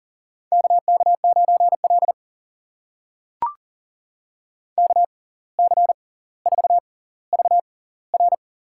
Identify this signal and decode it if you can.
Morse code, heard as a keyed tone, about 40 wpm, spaced as for 19 wpm, 700 Hz. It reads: KK9L KC4VR
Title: USA Call Signs - 6x Character Spacing - 40wpm